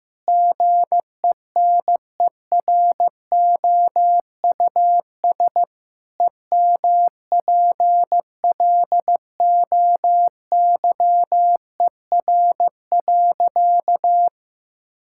Morse code, heard as a keyed tone, about 15 wpm, 700 Hz. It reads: GENEROUS EMPLOYER.
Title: Pangram Challenge v2 - 15wpm